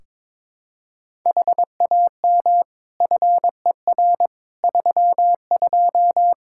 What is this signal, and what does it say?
Morse code, heard as a keyed tone, about 22 wpm, 700 Hz. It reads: HAM FER 32
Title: Practice QSO - Exchange 3 - More Info - 22wpm